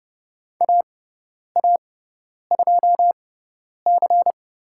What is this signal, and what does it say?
Morse code, heard as a keyed tone, about 30 wpm, spaced as for 10 wpm, 700 Hz. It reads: AA2C
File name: Lesson 18g - Letter m - Call Signs 30wpm